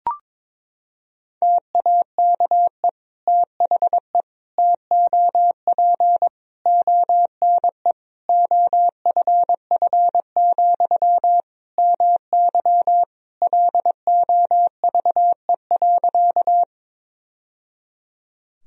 Morse code, heard as a keyed tone, about 22 wpm, 700 Hz. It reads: TAKE THE TOP ONE OFF, MY LOVE.
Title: Sentences from Top 1000 Words - Review Part 2 - 22wpm